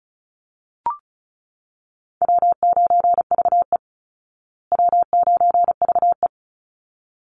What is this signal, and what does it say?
Morse code, heard as a keyed tone, about 34 wpm, 700 Hz. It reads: W9VE W9VE
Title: Easy Call Signs Repeated 2x - 35wpm